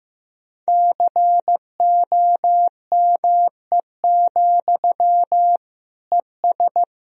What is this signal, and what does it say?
Morse code, heard as a keyed tone, about 16 wpm, 700 Hz. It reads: COME, ES
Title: Sentences from Top 500 Words and Abbreviations 15wpm